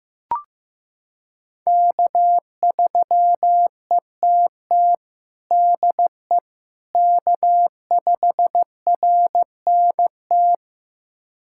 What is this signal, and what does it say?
Morse code, heard as a keyed tone, about 15 wpm, 700 Hz. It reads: K3ETT DE K5RNT
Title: USA Callsign DE Callsign - 5 Characters - 15wpm